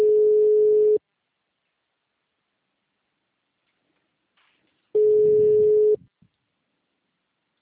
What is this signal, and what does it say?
Ringback tone (Europe, 425 Hz).